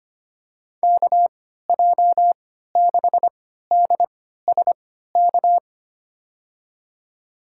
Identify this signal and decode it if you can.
Morse code, heard as a keyed tone, about 25 wpm, spaced as for 14 wpm, 700 Hz. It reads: KJ6DSK
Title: USA Call Signs - 3x Character Spacing - 25wpm